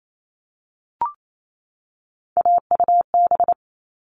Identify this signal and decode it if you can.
Morse code, heard as a keyed tone, about 28 wpm, 700 Hz. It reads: AUB
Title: All Cut Numbers - 3-Digit 28wpm